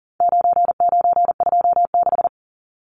Morse code, heard as a keyed tone, about 40 wpm, 700 Hz. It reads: 9926
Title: Numbers - 4-Digit 40wpm